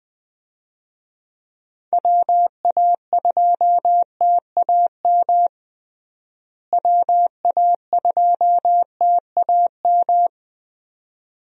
Morse code, heard as a keyed tone, about 20 wpm, 700 Hz. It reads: WA2TAM WA2TAM
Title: USA Call Signs - 6 Characters - Repeated 3x - 20wpm